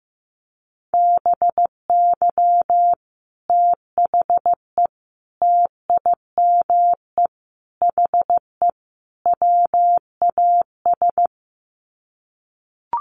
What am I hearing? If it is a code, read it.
Morse code, heard as a keyed tone, about 15 wpm, 700 Hz. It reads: BY THE TIME HE WAS
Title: Sets of 5 Words Encore 15wpm